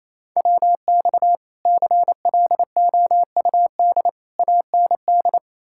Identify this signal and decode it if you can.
Morse code, heard as a keyed tone, about 38 wpm, spaced as for 33 wpm, 700 Hz. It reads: WX CLOUD AND
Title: Speed-Racing Common QSO Phrases - 28wpm